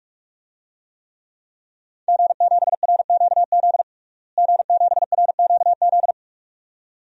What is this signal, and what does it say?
Morse code, heard as a keyed tone, about 41 wpm, 700 Hz. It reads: G7RQZ G7RQZ
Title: DXCC-100 Call Signs - 5 Characters - Repeated 2x - 45wpm